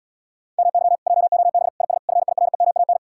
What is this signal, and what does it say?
Morse code, heard as a keyed tone, about 80 wpm, 700 Hz. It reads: MY JOB IS FIREMAN